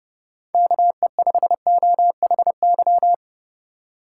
Morse code, heard as a keyed tone, about 31 wpm, 700 Hz. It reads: KE5OHY
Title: USA Call Signs - 6 Characters - 4x Character Spacing - 30wpm